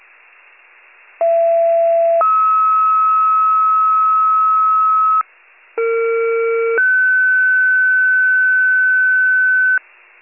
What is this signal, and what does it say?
Two-tone page, heard as a one-second tone, then a three-second tone, about 670 then 1290 Hz, about 460 then 1500 Hz. It is heard from 1.2 to 9.8 s.